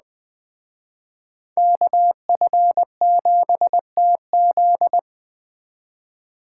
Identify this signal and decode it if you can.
Morse code, heard as a keyed tone, about 21 wpm, 700 Hz. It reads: KF7TZ